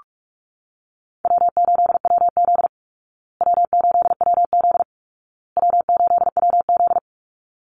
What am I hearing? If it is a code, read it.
Morse code, heard as a keyed tone, about 44 wpm, 700 Hz. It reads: W8WZ W8WZ W8WZ
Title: Easy Call Signs Repeated 3x - 45wpm